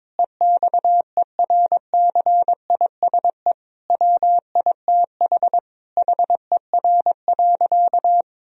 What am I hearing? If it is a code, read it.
Morse code, heard as a keyed tone, about 21 wpm, 700 Hz. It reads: EXERCISE WITH HER.
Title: Sentences from Top 1000 Words - Review Part 4 - 22wpm